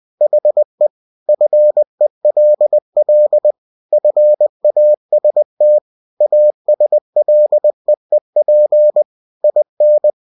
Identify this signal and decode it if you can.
Morse code, heard as a keyed tone, about 20 wpm, 600 Hz. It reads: HE FELL FAST ASLEEP IN
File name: cw-books / winnie-the-pooh / winnie-20wpm-page-0093